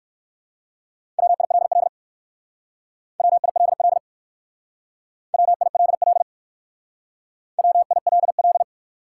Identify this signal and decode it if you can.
Morse code, heard as a keyed tone, about 56 wpm, 700 Hz. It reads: WILL WILL WILL WILL
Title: Speed-Racing Top 180 QSO Elements 45wpm